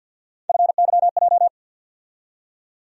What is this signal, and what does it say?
Morse code, heard as a keyed tone, about 46 wpm, 700 Hz. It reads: FYJ
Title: USA Call Sign Postfixes - 3 Letters 50wpm